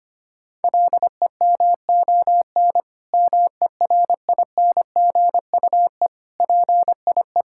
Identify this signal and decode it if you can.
Morse code, heard as a keyed tone, about 25 wpm, 700 Hz. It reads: LEMON MERINGUE PIE
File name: practice / 25wpm / Difficult Words with Hints - 25wpm